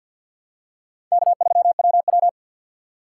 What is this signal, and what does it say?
Morse code, heard as a keyed tone, about 53 wpm, 700 Hz. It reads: K3WW
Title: USA Call Signs - 4x Character Spacing - 50wpm